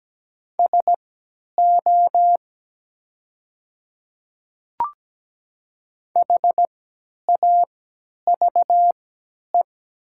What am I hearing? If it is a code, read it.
Morse code, heard as a keyed tone, about 17 wpm, spaced as for 10 wpm, 700 Hz. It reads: SO HAVE
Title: 3x Character Spacing - Top 100 Words - 17wpm